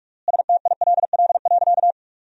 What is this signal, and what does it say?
Morse code, heard as a keyed tone, about 46 wpm, 700 Hz. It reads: STILL.
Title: 3x Word-Spacing - Sentences from Top 500 Words - 45wpm